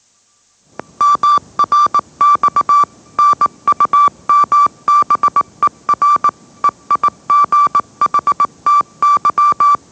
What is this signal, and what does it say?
Morse code, heard as a keyed tone, about 19 wpm, 1170 Hz. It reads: MRX NUMBER EIGHTY
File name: mrx / mrx081 20wpm